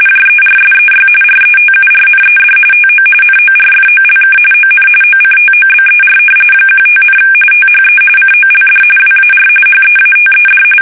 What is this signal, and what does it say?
RTTY, about 45.45 baud. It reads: JERSEY RETURNED TO NORFOLK ON OCTOBER 15, THE COLD WAR HEATED
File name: fsk850 / battleshipnj